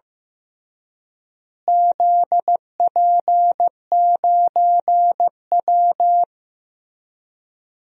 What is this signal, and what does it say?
Morse code, heard as a keyed tone, about 15 wpm, 700 Hz. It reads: ZP9W
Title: DXCC-100 Call Signs - 4 Characters - Repeated 3x - 15wpm